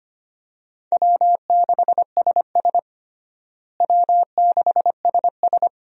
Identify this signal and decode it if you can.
Morse code, heard as a keyed tone, about 25 wpm, 700 Hz. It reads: W6SS W6SS
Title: USA Call Signs - 4 Characters - Repeated 2x - 25wpm